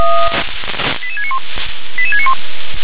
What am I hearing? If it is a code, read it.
Five-tone selective call (ZVEI), heard as quick runs of five tones, tones of about 70 ms. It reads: 88691 88691